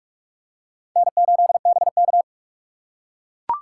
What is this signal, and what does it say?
Morse code, heard as a keyed tone, about 43 wpm, 700 Hz. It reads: N8BK
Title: USA Call Signs - 4 Characters - 4x Character Spacing - 45wpm